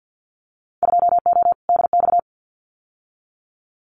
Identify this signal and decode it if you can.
Morse code, heard as a keyed tone, about 50 wpm, 700 Hz. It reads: 2O DX
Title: Winter Field Day Exchanges Encore 50wpm